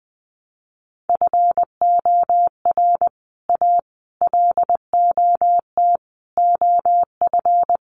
Morse code, heard as a keyed tone, about 20 wpm, 700 Hz. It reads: FOR A LOT OF
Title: Sets of 5 Words Encore 20wpm